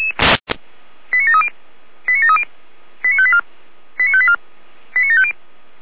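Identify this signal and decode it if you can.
Five-tone selective call (ZVEI), heard as quick runs of five tones, tones of about 70 ms. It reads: 89730 89730 89584 89584 89750
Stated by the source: Received with an SDR as ZVEI five-tone selective call, code 89725.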